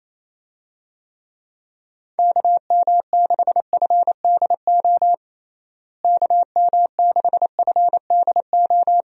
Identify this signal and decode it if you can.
Morse code, heard as a keyed tone, about 28 wpm, 700 Hz. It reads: KM6FDO KM6FDO
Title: USA Call Signs - 6 Characters - Repeated 2x - 28wpm